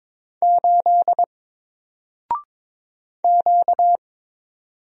Morse code, heard as a keyed tone, about 22 wpm, 700 Hz. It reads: 8 Q
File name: Single-Letter-Number-22wpm